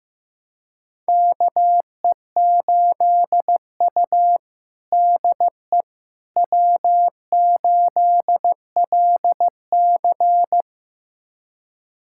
Morse code, heard as a keyed tone, about 15 wpm, 700 Hz. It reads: KE8U DE W8LC
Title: USA Callsign DE Callsign - 4 Characters - 15wpm